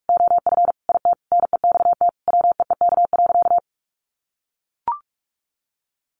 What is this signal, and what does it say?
Morse code, heard as a keyed tone, about 44 wpm, 700 Hz. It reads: OF IT NEXT WEEK.
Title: Sentences from Top 1400 Words 45wpm